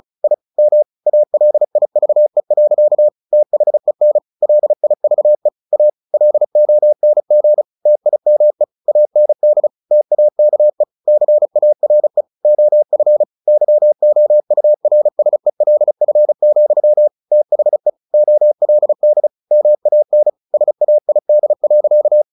Morse code, heard as a keyed tone, about 34 wpm, 600 Hz. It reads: I M ALIVE. THEN LIVE A LONG TIME AND TAKE CARE OF YOURSELF, THE OLD MAN SAID.
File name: the old man and the sea / the old man and the sea 1 35wpm